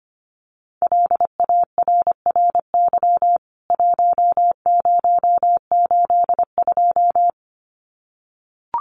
Morse code, heard as a keyed tone, about 25 wpm, 700 Hz. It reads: LARRY 1082
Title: CWT Exchange 25wpm